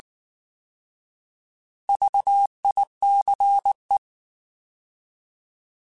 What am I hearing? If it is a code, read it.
Morse code, heard as a keyed tone, about 19 wpm, 800 Hz. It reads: VICE